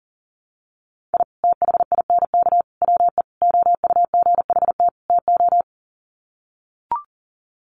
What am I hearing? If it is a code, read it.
Morse code, heard as a keyed tone, about 40 wpm, 700 Hz. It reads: I THINK WE OUGHT TO
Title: Sets of 5 Words Encore 40wpm